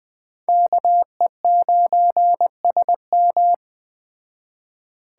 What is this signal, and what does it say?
Morse code, heard as a keyed tone, about 21 wpm, 700 Hz. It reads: KE9SM